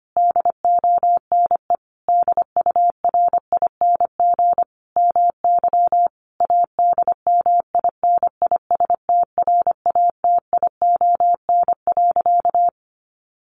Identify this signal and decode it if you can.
Morse code, heard as a keyed tone, about 25 wpm, 700 Hz. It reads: DONE DURING MY ADMINISTRATION.
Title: Sentences from Top 1300 Words 25wpm